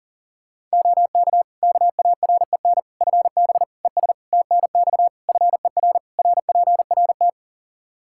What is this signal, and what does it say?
Morse code, heard as a keyed tone, about 40 wpm, 700 Hz. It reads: OK KAREN FB ES TNX FER RPRT
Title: Practice QSO - Exchange 2 - Info - 40wpm